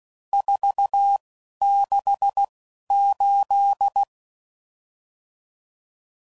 Morse code, heard as a keyed tone, about 16 wpm, spaced as for 11 wpm, 800 Hz. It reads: 468
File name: CW Code Groups @10WPM/RSTs/Various RST No's